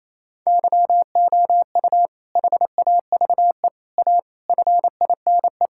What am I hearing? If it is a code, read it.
Morse code, heard as a keyed tone, about 28 wpm, 700 Hz. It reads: YOU HAVE A FINE
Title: Sentences from Top 1000 Words - Review Part 6 - 28wpm